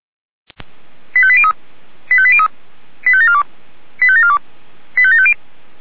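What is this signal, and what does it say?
Five-tone selective call (ZVEI), heard as quick runs of five tones, tones of about 70 ms. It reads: 85803 85803 85742 85742 85750
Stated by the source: ZVEI five-tone call received by SDR.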